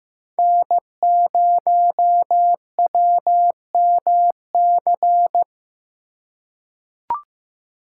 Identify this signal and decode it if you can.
Morse code, heard as a keyed tone, about 15 wpm, 700 Hz. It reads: N0WMC
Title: USA Call Signs - 5 Characters - 2x Character Spacing - 15wpm